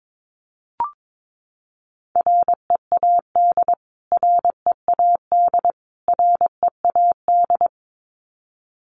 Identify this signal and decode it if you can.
Morse code, heard as a keyed tone, about 22 wpm, 700 Hz. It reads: READ READ READ
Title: Top 500 Words Repeated 22wpm